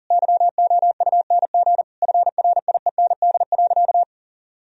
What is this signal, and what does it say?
Morse code, heard as a keyed tone, about 40 wpm, 700 Hz. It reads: YOUNG FRIEND.
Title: Sentences from Top 1000 Words - Review Part 3 - 40wpm